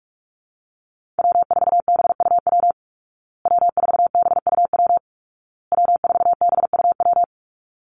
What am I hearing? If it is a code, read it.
Morse code, heard as a keyed tone, about 44 wpm, 700 Hz. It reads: W4BUW W4BUW W4BUW
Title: Call Signs Repeated 2x - 45wpm